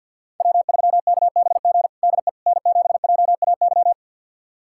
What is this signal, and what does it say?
Morse code, heard as a keyed tone, about 52 wpm, 700 Hz. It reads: W3KBG DE N7JAY
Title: USA Callsign DE Callsign - 5 Characters - 50wpm